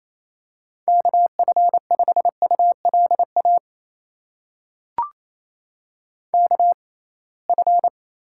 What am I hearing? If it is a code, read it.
Morse code, heard as a keyed tone, about 28 wpm, spaced as for 21 wpm, 700 Hz. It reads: KF5ULA KF
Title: USA Call Signs - 6 Characters - 6x Character Spacing - 28wpm